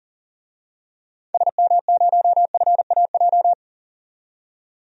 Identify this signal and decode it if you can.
Morse code, heard as a keyed tone, about 40 wpm, 700 Hz. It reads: SM0FAJ
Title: DXCC-100 Call Signs - 6 Characters - 40wpm